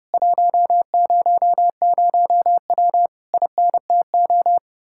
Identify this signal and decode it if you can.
Morse code, heard as a keyed tone, about 33 wpm, 700 Hz. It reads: 100W INTO